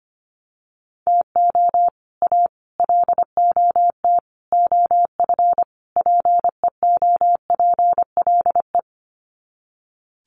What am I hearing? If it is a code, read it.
Morse code, heard as a keyed tone, about 25 wpm, 700 Hz. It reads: TO A LOT OF PEOPLE
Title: Sets of 5 Words Encore 25wpm